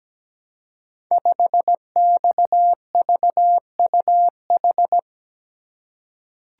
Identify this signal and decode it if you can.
Morse code, heard as a keyed tone, about 17 wpm, 700 Hz. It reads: 5XVUH
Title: Code Groups - 5 Letter:Numbers - 17wpm